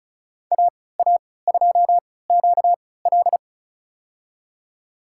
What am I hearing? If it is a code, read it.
Morse code, heard as a keyed tone, about 34 wpm, spaced as for 20 wpm, 700 Hz. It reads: AA2QL
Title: USA Call Signs - 5 Characters - 3x Character Spacing - 35wpm